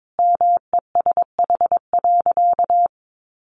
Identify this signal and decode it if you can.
Morse code, heard as a keyed tone, about 22 wpm, 700 Hz. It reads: MESH.
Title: A Sound Of Thunder 22wpm